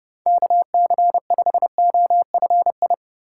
Morse code, heard as a keyed tone, about 30 wpm, 700 Hz. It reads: KC5OFI
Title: USA Call Signs - 6 Characters - 4x Character Spacing - 30wpm